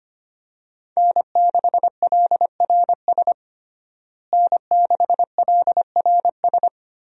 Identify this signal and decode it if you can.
Morse code, heard as a keyed tone, about 25 wpm, 700 Hz. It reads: N6LRS N6LRS
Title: USA Call Signs - 5 Characters - Repeated 3x - 25wpm